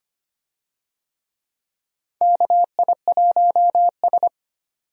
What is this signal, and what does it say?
Morse code, heard as a keyed tone, about 25 wpm, 700 Hz. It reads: KI1S
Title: USA Call Signs - 4 Characters - 25wpm